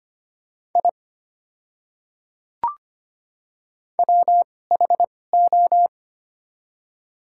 Morse code, heard as a keyed tone, about 25 wpm, spaced as for 18 wpm, 700 Hz. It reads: I WHO
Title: 2x Character Spacing - Top 200 Words - 25wpm